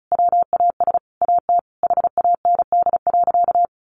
Morse code, heard as a keyed tone, about 34 wpm, 700 Hz. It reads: WAS AT HAND.
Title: Sentences from Top 1300 Words 35wpm